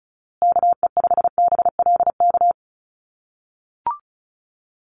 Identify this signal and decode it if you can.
Morse code, heard as a keyed tone, about 34 wpm, 700 Hz. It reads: KE5BLK
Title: USA Call Signs Repeated 2x 35wpm